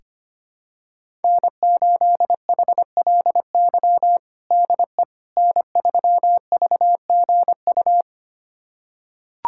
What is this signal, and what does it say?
Morse code, heard as a keyed tone, about 25 wpm, 700 Hz. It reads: N8HLY DE N3VGU
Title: USA Callsign DE Callsign - 5 Characters - 25wpm